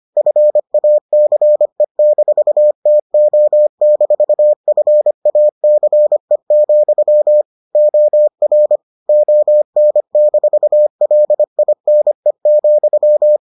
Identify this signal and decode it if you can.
Morse code, heard as a keyed tone, about 25 wpm, 600 Hz. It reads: FACE-TO-FACE, OR ON-LINE,